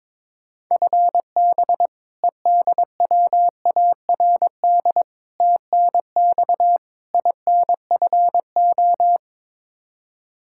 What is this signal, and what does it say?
Morse code, heard as a keyed tone, about 22 wpm, 700 Hz. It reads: FB EDWARD TNX INFO
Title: Practice QSO - Exchange 3 - More Info - 22wpm